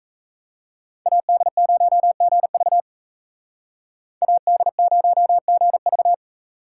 Morse code, heard as a keyed tone, about 40 wpm, 700 Hz. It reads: AD0GV AD0GV